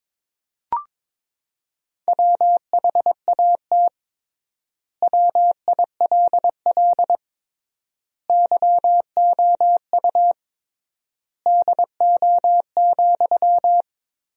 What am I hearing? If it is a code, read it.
Morse code, heard as a keyed tone, about 22 wpm, 700 Hz. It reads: WHAT WILL YOU DO,
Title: 3x Word-Spacing - Sentences from Top 100 Words - 22wpm